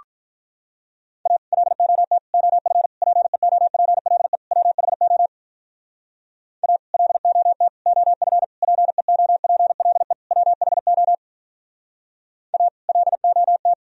Morse code, heard as a keyed tone, about 46 wpm, 700 Hz. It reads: A LOT OF PEOPLE WHO A LOT OF PEOPLE WHO A LOT